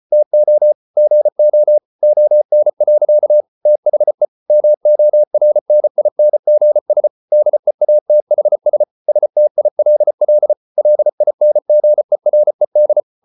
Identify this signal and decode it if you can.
Morse code, heard as a keyed tone, about 34 wpm, 600 Hz. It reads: TO GO ON. THE MORNINGS DEATHS STILL LINGERED